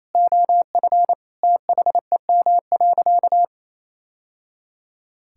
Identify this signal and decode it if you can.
Morse code, heard as a keyed tone, about 28 wpm, 700 Hz. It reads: OF THEM.
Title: Sentences from Top 1000 Words - Review Part 5 - 28wpm